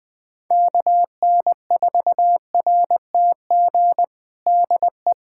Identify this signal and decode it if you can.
Morse code, heard as a keyed tone, about 21 wpm, 700 Hz. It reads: KN4RTG DE
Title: USA Callsign DE Callsign - 6 Characters - 20wpm